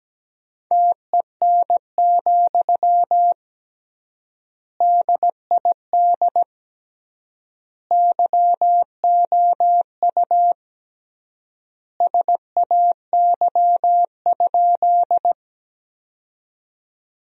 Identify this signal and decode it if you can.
Morse code, heard as a keyed tone, about 17 wpm, 700 Hz. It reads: TEN, DID YOU SAY?
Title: 3x Word-Spacing - Sentences from Top 400 Words - 17wpm